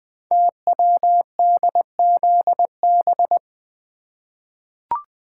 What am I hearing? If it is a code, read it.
Morse code, heard as a keyed tone, about 20 wpm, 700 Hz. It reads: TWDZB
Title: Code Groups - 5 Letters - 20wpm